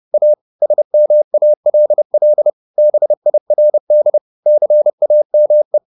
Morse code, heard as a keyed tone, about 30 wpm, 600 Hz. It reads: A SMALL BIRD CAME
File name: the old man and the sea / the old man and the sea 1 30wpm